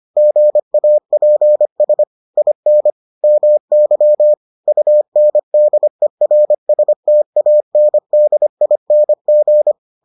Morse code, heard as a keyed tone, about 25 wpm, 600 Hz. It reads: GAPS IN MY UNDERSTANDING